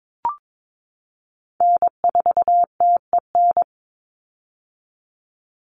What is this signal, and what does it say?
Morse code, heard as a keyed tone, about 22 wpm, 700 Hz. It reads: N4TEN
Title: USA Call Signs 22wpm